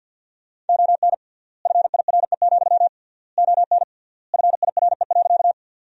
Morse code, heard as a keyed tone, about 52 wpm, 700 Hz. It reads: ON FIRE, ON FIRE.
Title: 3x Word-Spacing - Sentences from Top 400 Words - 50wpm